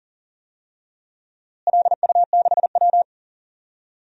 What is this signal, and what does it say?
Morse code, heard as a keyed tone, about 38 wpm, 700 Hz. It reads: LU6W
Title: DXCC-100 Call Signs - 4 Characters - Repeated 3x - 40wpm